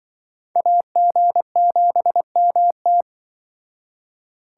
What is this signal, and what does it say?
Morse code, heard as a keyed tone, about 25 wpm, 700 Hz. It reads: AG7MT
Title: Speed Racing - USA 2x2 Call Signs 20wpm